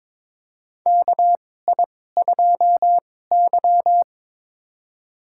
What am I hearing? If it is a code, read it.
Morse code, heard as a keyed tone, about 22 wpm, spaced as for 16 wpm, 700 Hz. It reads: KI2Y
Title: USA Call Signs - 4 Characters - 2x Character Spacing - 22wpm